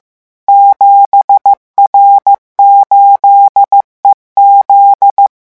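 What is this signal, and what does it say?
Morse code, heard as a keyed tone, about 15 wpm, 800 Hz. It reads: 7R8EZ
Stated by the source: Morse revision lesson at 14 words per minute.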